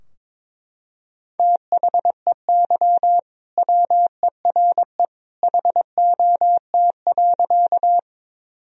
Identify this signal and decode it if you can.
Morse code, heard as a keyed tone, about 21 wpm, 700 Hz. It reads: THEY WERE HOT.
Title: Sentences from Top 1000 Words - Review Part 3 - 22wpm